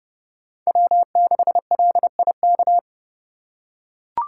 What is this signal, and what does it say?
Morse code, heard as a keyed tone, about 30 wpm, 690 Hz. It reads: W6LIK
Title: USA Call Signs - 3x Character Spacing - 30wpm